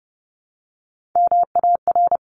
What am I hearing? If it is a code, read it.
Morse code, heard as a keyed tone, about 30 wpm, 700 Hz. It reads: MAR
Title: ARRL Sections 30wpm